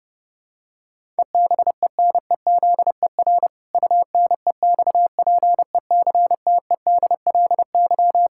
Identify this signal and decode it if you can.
Morse code, heard as a keyed tone, about 29 wpm, 700 Hz. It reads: EBENEZER UNEXPECTEDLY
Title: Pangram Challenge v2 - 30wpm